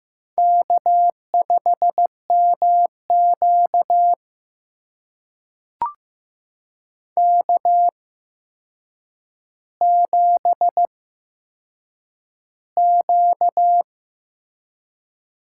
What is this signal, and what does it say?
Morse code, heard as a keyed tone, about 15 wpm, spaced as for 7 wpm, 700 Hz. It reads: K5MQ K7Q
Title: USA Call Signs - 4 Characters - 8x Character Spacing - 15wpm